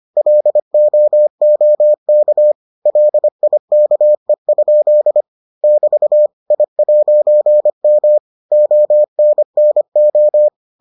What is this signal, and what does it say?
Morse code, heard as a keyed tone, about 25 wpm, 600 Hz. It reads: LOOK LIKE? <BT> I'M ONNO